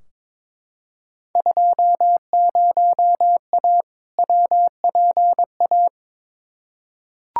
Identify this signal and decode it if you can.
Morse code, heard as a keyed tone, about 22 wpm, 700 Hz. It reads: 20A WPA
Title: Field Day Exchanges Encore - 22wpm